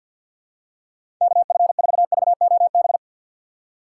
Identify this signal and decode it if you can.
Morse code, heard as a keyed tone, about 52 wpm, 700 Hz. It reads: KF4VOB
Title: USA Call Signs - 4x Character Spacing - 50wpm